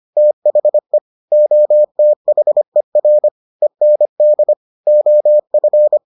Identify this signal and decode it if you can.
Morse code, heard as a keyed tone, about 25 wpm, 600 Hz. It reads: THE OTHER END OF